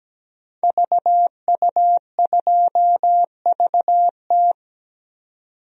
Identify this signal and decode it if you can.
Morse code, heard as a keyed tone, about 17 wpm, 700 Hz. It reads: VU2VT
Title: DXCC-100 Call Signs - 5 Characters - 17wpm